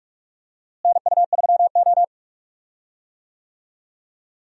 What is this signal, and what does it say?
Morse code, heard as a keyed tone, about 44 wpm, 700 Hz. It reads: NU3O
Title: USA Call Signs - 4 Characters - Repeated 3x - 45wpm